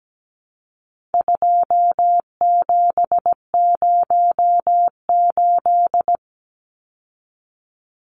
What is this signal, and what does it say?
Morse code, heard as a keyed tone, about 17 wpm, 700 Hz. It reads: 2708
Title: Numbers - 4-Digit 17wpm